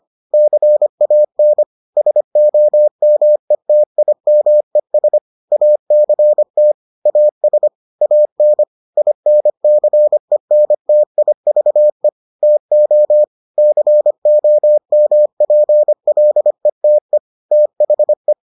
Morse code, heard as a keyed tone, about 25 wpm, 600 Hz. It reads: CAN SOMETIMES ACT AS AN INCENTIVE TO COMPLETE THE